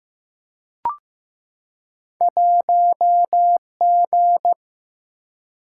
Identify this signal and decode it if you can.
Morse code, heard as a keyed tone, about 15 wpm, 700 Hz. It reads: 1G
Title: Single-Letter Single-Number 15wpm